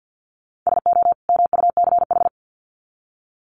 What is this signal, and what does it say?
Morse code, heard as a keyed tone, about 50 wpm, 700 Hz. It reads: SO MUCH
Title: Sets of 2 Words 50wpm